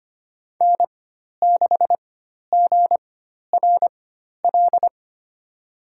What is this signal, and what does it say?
Morse code, heard as a keyed tone, about 25 wpm, spaced as for 12 wpm, 700 Hz. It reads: N6GRL
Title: USA Call Signs - 5 Characters - 4x Character Spacing - 25wpm